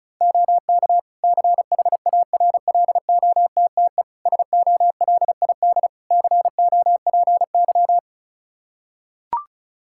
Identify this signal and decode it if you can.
Morse code, heard as a keyed tone, about 36 wpm, 700 Hz. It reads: OK CHARLOTTE SOLID COPY
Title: Practice QSO - Exchange 3 - More Info - 35wpm